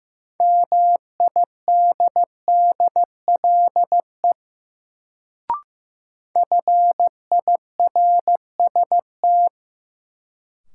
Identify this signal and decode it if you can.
Morse code, heard as a keyed tone, about 16 wpm, 700 Hz. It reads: MIDDLE FIRST
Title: Sets of 2 Words - New Word Drill - 15wpm